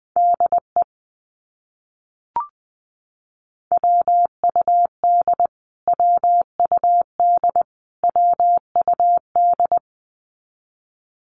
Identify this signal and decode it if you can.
Morse code, heard as a keyed tone, about 20 wpm, 700 Hz. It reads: DE WUD WUD WUD
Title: Abbreviations Repeated 3x - 20wpm